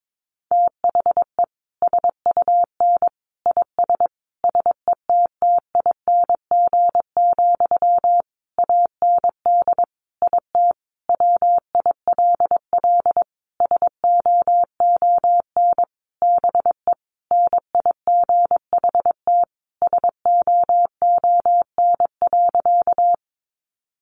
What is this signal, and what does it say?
Morse code, heard as a keyed tone, about 22 wpm, 700 Hz. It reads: THE SUN IS SETTING, AND IT WILL SOON BE NIGHT SOON.
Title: Sentences from Top 1300 Words 22wpm